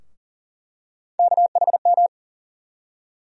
Morse code, heard as a keyed tone, about 40 wpm, 700 Hz. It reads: KHM
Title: USA Call Sign Postfixes - 3 Letters 40wpm